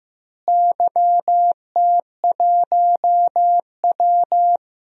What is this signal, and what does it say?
Morse code, heard as a keyed tone, about 15 wpm, 700 Hz. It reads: YT1W
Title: DXCC-100 Call Signs - 4 Characters - Repeated 2x - 15wpm